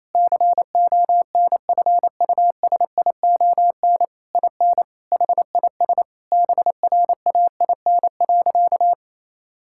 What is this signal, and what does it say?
Morse code, heard as a keyed tone, about 28 wpm, 700 Hz. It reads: CONFUSION IN HIS BRAIN.
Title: Sentences from Top 1600 Words 28wpm